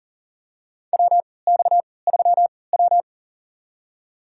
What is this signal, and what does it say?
Morse code, heard as a keyed tone, about 40 wpm, spaced as for 23 wpm, 700 Hz. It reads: WX3W
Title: USA Call Signs - 4 Characters - 3x Character Spacing - 40wpm